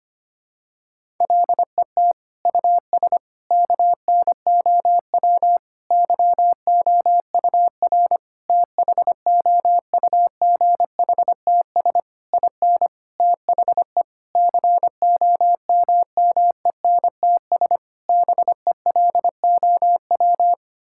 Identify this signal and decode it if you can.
Morse code, heard as a keyed tone, about 25 wpm, 700 Hz. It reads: LET US KNOW YOUR THOUGHTS IN THE COMMENTS BELOW
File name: Sets of 9 Words - 25wpm